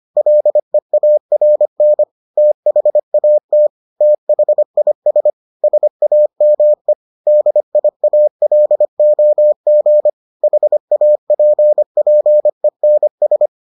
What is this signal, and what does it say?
Morse code, heard as a keyed tone, about 25 wpm, 600 Hz. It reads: LEARN THAT THIS SAME DIALOG HAPPENS